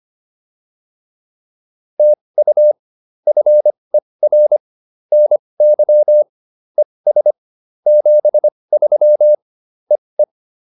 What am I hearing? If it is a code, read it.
Morse code, heard as a keyed tone, about 25 wpm, spaced as for 20 wpm, 600 Hz. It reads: TU FER NY ES 73 EE
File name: POTA210a 20